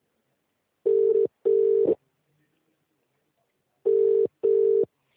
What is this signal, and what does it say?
Ringback tone (UK), 400 + 450 Hz.